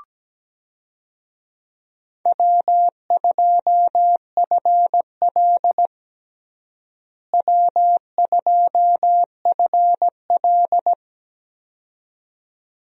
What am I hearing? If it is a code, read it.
Morse code, heard as a keyed tone, about 17 wpm, 700 Hz. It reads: W2FL W2FL
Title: USA Call Signs - 4 Characters - Repeated 2x - 17wpm